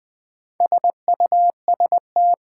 Morse code, heard as a keyed tone, about 20 wpm, 700 Hz. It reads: SUST
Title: Build A Word - 4 Letters At A Time - 11-Letter Words - 20wpm